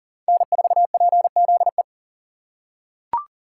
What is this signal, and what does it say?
Morse code, heard as a keyed tone, about 40 wpm, 700 Hz. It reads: N4PZE
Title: USA Call Signs - 5 Characters - 3x Character Spacing - 40wpm